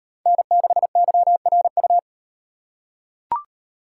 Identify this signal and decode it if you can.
Morse code, heard as a keyed tone, about 38 wpm, 700 Hz. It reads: N6YRU